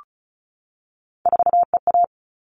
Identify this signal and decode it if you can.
Morse code, heard as a keyed tone, about 34 wpm, 700 Hz. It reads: 4EA
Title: All Cut Numbers - 3-Digit 35wpm